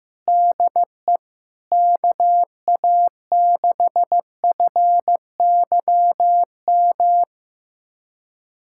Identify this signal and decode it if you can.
Morse code, heard as a keyed tone, about 15 wpm, 700 Hz. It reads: DE KA6FYM
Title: USA Callsign DE Callsign - 6 Characters - 15wpm